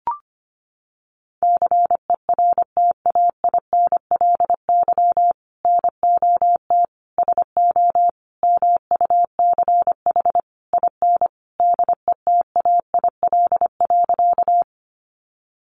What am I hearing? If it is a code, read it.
Morse code, heard as a keyed tone, about 25 wpm, 700 Hz. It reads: CERTAINLY NOT SO MUCH IN DETAIL.
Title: Sentences from Top 1500 Words 25wpm